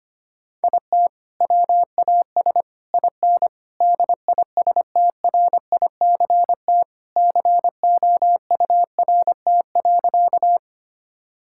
Morse code, heard as a keyed tone, about 25 wpm, 700 Hz. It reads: IT WAS IN DISTRICT COURT.
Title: Sentences from Top 1600 Words 25wpm